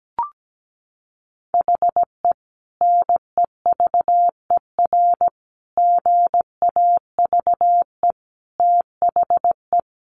Morse code, heard as a keyed tone, about 17 wpm, 700 Hz. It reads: HE NEVER GAVE THE
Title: Sentences from Top 1200 Words 17wpm